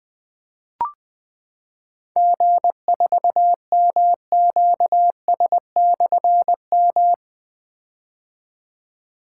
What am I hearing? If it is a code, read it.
Morse code, heard as a keyed tone, about 20 wpm, 700 Hz. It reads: G4MQS/M
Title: Difficult Call Signs 20wpm